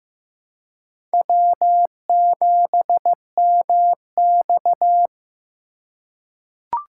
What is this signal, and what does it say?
Morse code, heard as a keyed tone, about 16 wpm, 700 Hz. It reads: W7MX
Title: USA Call Signs - 4 Characters - 2x Character Spacing - 15wpm